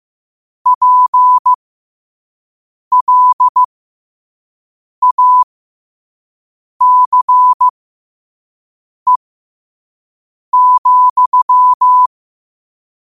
Morse code, heard as a keyed tone, about 15 wpm, spaced as for 5 wpm, 1000 Hz. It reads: PLACE,
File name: qotdfiles / 5wpm